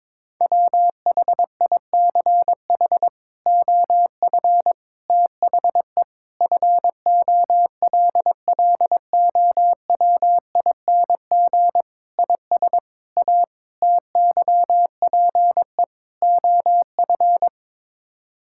Morse code, heard as a keyed tone, about 22 wpm, 700 Hz. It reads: WHICH OF THE FOLLOWING IS A TYPE OF
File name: practice / 22wpm / Sets of 8 Words - 22wpm